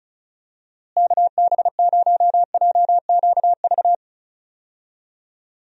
Morse code, heard as a keyed tone, about 36 wpm, 700 Hz. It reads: KB0JQV
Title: USA Call Signs - 6 Characters - Repeated 3x - 35wpm